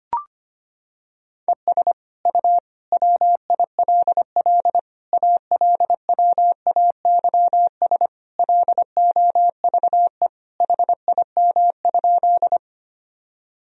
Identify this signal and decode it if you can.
Morse code, heard as a keyed tone, about 25 wpm, 700 Hz. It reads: ES U WILL ALWAYS LOVE HIM?
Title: Sentences from Top 400 Words and Abbreviations 25wpm